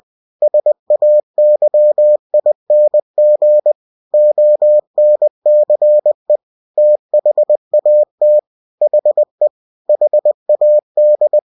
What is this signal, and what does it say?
Morse code, heard as a keyed tone, about 20 wpm, 600 Hz. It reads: SAYING ONCE THAT HE HAD
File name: cw-books / winnie-the-pooh / winnie-20wpm-page-0037